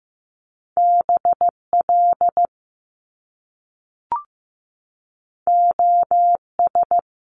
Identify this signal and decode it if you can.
Morse code, heard as a keyed tone, about 15 wpm, 700 Hz. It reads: BL OS
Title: Top Two Letter Combinations in Words 15wpm